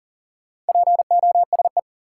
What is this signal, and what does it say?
Morse code, heard as a keyed tone, about 40 wpm, 700 Hz. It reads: POSE
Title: Sentences from Top 1000 Words - Review Part 2 - New Word Drill - 40wpm